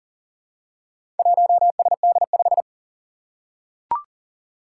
Morse code, heard as a keyed tone, about 41 wpm, 700 Hz. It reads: 1SD5
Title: Four Challenging Letter-Number Combinations 40wpm